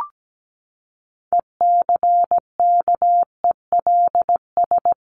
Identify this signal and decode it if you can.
Morse code, heard as a keyed tone, about 17 wpm, 700 Hz. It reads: ECKELS
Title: A Sound Of Thunder 17wpm